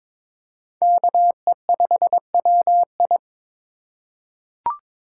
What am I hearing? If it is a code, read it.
Morse code, heard as a keyed tone, about 22 wpm, 700 Hz. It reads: KE5WI
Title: USA Call Signs - 5 Characters - 2x Character Spacing - 22wpm